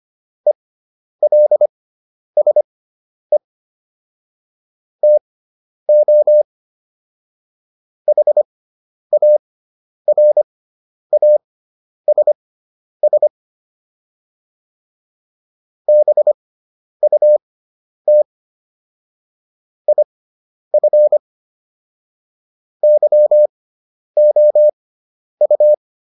Morse code, heard as a keyed tone, about 25 wpm, spaced as for 10 wpm, 600 Hz. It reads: ELSE TO HARASS BUT IF YOU